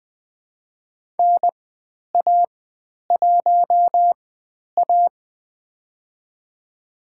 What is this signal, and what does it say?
Morse code, heard as a keyed tone, about 20 wpm, spaced as for 10 wpm, 700 Hz. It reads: NA1A
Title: 102 Lesson 17g - Letter u - Call Signs 20wpm